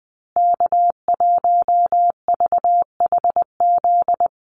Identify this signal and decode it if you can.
Morse code, heard as a keyed tone, about 20 wpm, 700 Hz. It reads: K1VHZ
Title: USA Call Signs Repeated 2x 20wpm